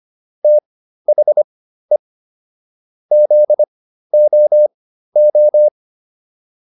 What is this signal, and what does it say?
Morse code, heard as a keyed tone, about 25 wpm, spaced as for 13 wpm, 600 Hz. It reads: THE ZOO
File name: PR402 13